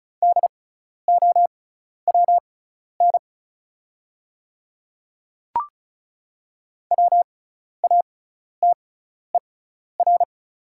Morse code, heard as a keyed tone, about 36 wpm, spaced as for 12 wpm, 700 Hz. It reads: DOWN WATER
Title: 6x Character Spacing - Top 100 Words - 35wpm